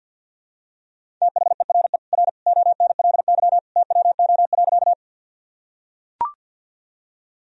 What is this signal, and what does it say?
Morse code, heard as a keyed tone, about 45 wpm, 700 Hz. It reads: THERE R ONLY TWO.
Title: Sentences from Top 200 Words and Abbreviations 50wpm